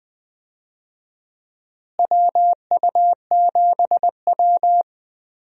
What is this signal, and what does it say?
Morse code, heard as a keyed tone, about 20 wpm, 700 Hz. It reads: WU7W
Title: USA Call Signs - 4 Characters - Repeated 3x - 20wpm